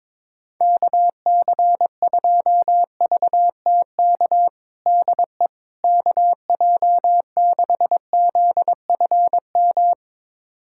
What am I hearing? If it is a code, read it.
Morse code, heard as a keyed tone, about 22 wpm, 700 Hz. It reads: KC2VTK DE KJ6ZFM
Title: USA Callsign DE Callsign 22wpm